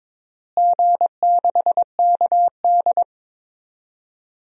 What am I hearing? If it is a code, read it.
Morse code, heard as a keyed tone, about 22 wpm, 700 Hz. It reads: G6KD
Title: DXCC-100 Call Signs - 4 Characters - Repeated 3x - 22wpm